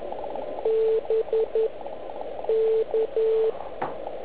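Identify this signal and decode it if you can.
Morse code, heard as a keyed tone, about 11 wpm, spaced as for 7 wpm, 440 Hz, with radio noise. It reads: BK
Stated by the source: Slow (QRS) CW test by beginner OK1IFW during beginner Morse skeds on 3559.5 kHz (80 m).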